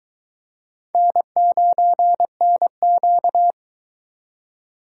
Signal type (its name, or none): Morse code